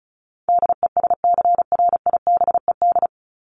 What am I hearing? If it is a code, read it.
Morse code, heard as a keyed tone, about 34 wpm, 700 Hz. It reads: DESCRIBED
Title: ED Suffix - Top 100 Words with Base Words 35wpm